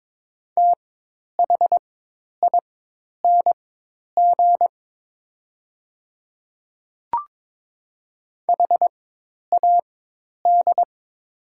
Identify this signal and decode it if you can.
Morse code, heard as a keyed tone, about 22 wpm, spaced as for 10 wpm, 700 Hz. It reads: THING HAD
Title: 4x Character Spacing - Top 100 Words - 22wpm